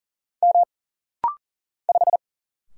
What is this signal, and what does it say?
Morse code, heard as a keyed tone, about 40 wpm, 700 Hz. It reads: M 5
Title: Single Letter-Number - Rapid-Fire 40wpm